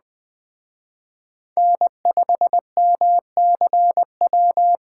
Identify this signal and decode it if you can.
Morse code, heard as a keyed tone, about 20 wpm, 700 Hz. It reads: N5MCW
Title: Speed Racing - USA Call Signs 17wpm